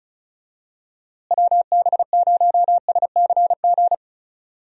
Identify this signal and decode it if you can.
Morse code, heard as a keyed tone, about 34 wpm, 700 Hz. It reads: WB0SCG
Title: USA Call Signs - 6 Characters - Repeated 2x - 35wpm